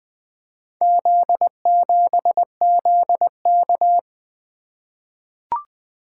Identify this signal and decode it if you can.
Morse code, heard as a keyed tone, about 20 wpm, 700 Hz. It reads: Z7ZK
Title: Four Letter-Number Combinations - 20wpm